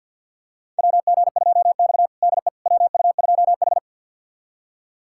Morse code, heard as a keyed tone, about 47 wpm, 700 Hz. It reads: WG2X DE WU2H
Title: USA Callsign DE Callsign - 4 Characters - 50wpm